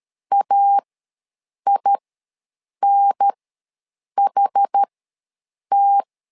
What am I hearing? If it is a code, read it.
Morse code, heard as a keyed tone, about 13 wpm, spaced as for 7 wpm, 800 Hz. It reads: AINHT
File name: LEZIONI / 2 UHDN / 2 UHDN ws 12wpm 1708